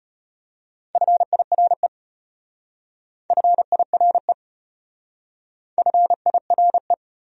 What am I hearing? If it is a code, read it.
Morse code, heard as a keyed tone, about 32 wpm, 700 Hz. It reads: FIRE FIRE FIRE